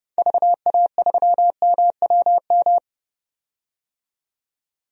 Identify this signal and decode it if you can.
Morse code, heard as a keyed tone, about 30 wpm, 700 Hz. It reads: VA3MWM
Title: DXCC-100 Call Signs - 6 Characters - 30wpm